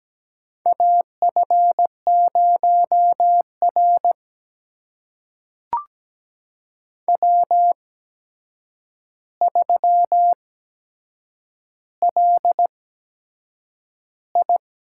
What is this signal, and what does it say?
Morse code, heard as a keyed tone, about 17 wpm, spaced as for 7 wpm, 700 Hz. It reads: AF0R W3LI
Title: USA Call Signs - 8x Character Spacing - 17wpm